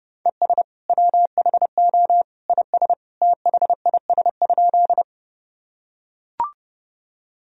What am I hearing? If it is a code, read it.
Morse code, heard as a keyed tone, about 30 wpm, 700 Hz. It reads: ES WHO IS THIS?
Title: Sentences from Top 100 Words and Abbreviations 30wpm